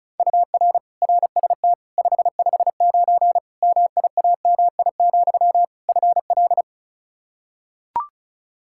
Morse code, heard as a keyed tone, about 34 wpm, 700 Hz. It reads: UR RST 559 MIAMI, FL
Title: K1USN SST QSO Practice 35wpm